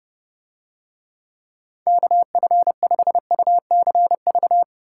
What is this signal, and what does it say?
Morse code, heard as a keyed tone, about 28 wpm, 700 Hz. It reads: KF5UCV